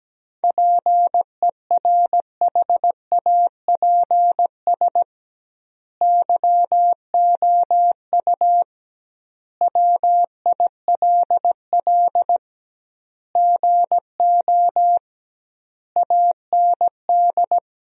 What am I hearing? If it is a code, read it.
Morse code, heard as a keyed tone, about 17 wpm, 700 Hz. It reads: PERHAPS YOU WILL GO AND
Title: 2x Word-Spacing - Sentences from Top 500 Words - 17wpm